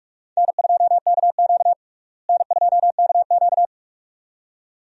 Morse code, heard as a keyed tone, about 46 wpm, 700 Hz. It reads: N2KQ N2KQ
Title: USA Call Signs - 4 Characters - Repeated 2x - 45wpm